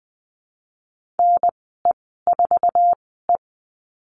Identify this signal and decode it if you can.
Morse code, heard as a keyed tone, about 20 wpm, spaced as for 14 wpm, 700 Hz. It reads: NE4E
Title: 055 Lesson 10g - Letter r - Call Signs 20wpm